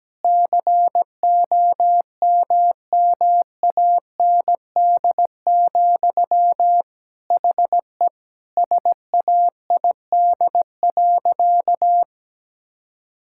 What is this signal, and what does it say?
Morse code, heard as a keyed tone, about 17 wpm, 700 Hz. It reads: COMMAND, HE SAID.
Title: Sentences from Top 1600 Words 17wpm